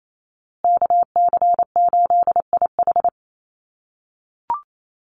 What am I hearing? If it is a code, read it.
Morse code, heard as a keyed tone, about 28 wpm, 700 Hz. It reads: KC8IH
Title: USA Call Signs Repeated 2x 28wpm